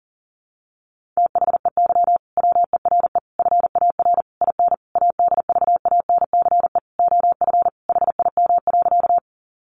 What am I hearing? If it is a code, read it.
Morse code, heard as a keyed tone, about 40 wpm, 700 Hz. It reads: THEY WERE FAR IN ADVANCE OF HIM.
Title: Sentences from Top 1400 Words 40wpm